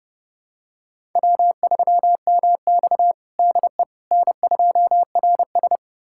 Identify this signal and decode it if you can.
Morse code, heard as a keyed tone, about 32 wpm, 700 Hz. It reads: W3MX DE N2RS
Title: USA Callsign DE Callsign - 4 Characters - 30wpm